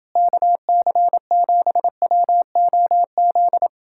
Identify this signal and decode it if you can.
Morse code, heard as a keyed tone, about 30 wpm, 700 Hz. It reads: KC7WOZ